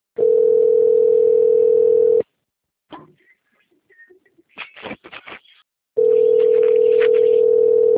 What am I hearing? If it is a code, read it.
Ringback tone (North America), 440 + 480 Hz.